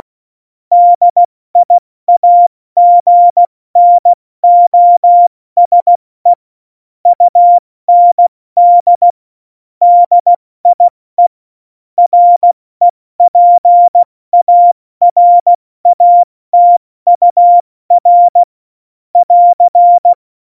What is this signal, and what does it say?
Morse code, heard as a keyed tone, about 16 wpm, 700 Hz. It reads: DIAGNOSE UND DIE REPARATUR <AR>
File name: Morsekurs / Witze / Witz 07 Juli 2020, 70 F80